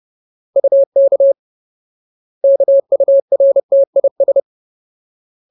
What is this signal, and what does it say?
Morse code, heard as a keyed tone, about 30 wpm, 550 Hz. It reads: UK KURTIS
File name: Ultimate Word List / 30WPM Ultimate